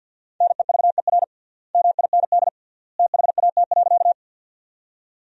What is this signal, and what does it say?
Morse code, heard as a keyed tone, about 47 wpm, 700 Hz. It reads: NEVER MIND THAT.
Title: 3x Word-Spacing - Sentences from Top 500 Words - 50wpm